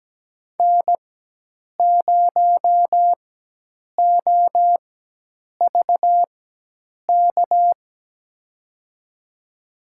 Morse code, heard as a keyed tone, about 17 wpm, spaced as for 8 wpm, 700 Hz. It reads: N0OVK
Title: USA Call Signs - 4x Character Spacing - 17wpm